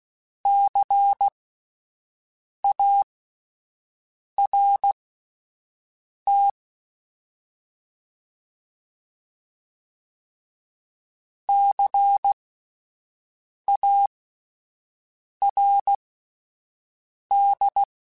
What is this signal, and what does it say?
Morse code, heard as a keyed tone, about 16 wpm, spaced as for 5 wpm, 800 Hz. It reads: CART CARD